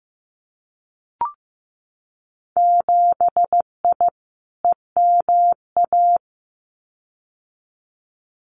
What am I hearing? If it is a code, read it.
Morse code, heard as a keyed tone, about 15 wpm, 700 Hz. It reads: 7I EMA
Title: Winter Field Day Exchanges Encore 15wpm